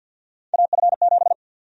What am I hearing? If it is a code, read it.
Morse code, heard as a keyed tone, about 51 wpm, 700 Hz. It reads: AF7
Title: USA Call Sign Prefixes - 50wpm